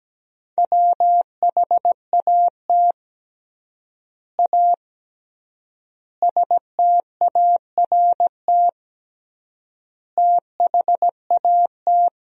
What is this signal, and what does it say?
Morse code, heard as a keyed tone, about 17 wpm, 700 Hz. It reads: WHAT A START THAT
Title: 3x Word-Spacing - Sentences from Top 300 Words - 17wpm